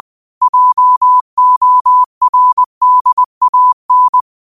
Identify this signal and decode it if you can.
Morse code, heard as a keyed tone, about 20 wpm, 1000 Hz. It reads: JORDAN
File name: qotdfiles / 20wpm